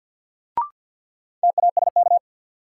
Morse code, heard as a keyed tone, about 50 wpm, 700 Hz. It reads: TASK
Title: Sentences from Top 1100 Words - New Word Drill Alt - 50wpm